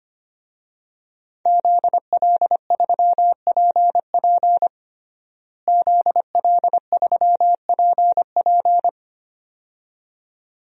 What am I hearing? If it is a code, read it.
Morse code, heard as a keyed tone, about 25 wpm, 700 Hz. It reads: ZL3PP ZL3PP
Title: DXCC-100 Call Signs - 5 Characters - Repeated 2x - 25wpm